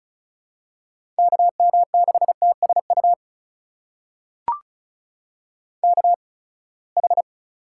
Morse code, heard as a keyed tone, about 36 wpm, spaced as for 24 wpm, 700 Hz. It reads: KM6TSU KH
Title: USA Call Signs - 8x Character Spacing - 35wpm